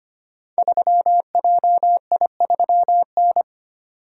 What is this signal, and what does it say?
Morse code, heard as a keyed tone, about 25 wpm, 700 Hz. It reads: 3JI3N
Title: Code Groups - 5 Letter:Numbers - 25wpm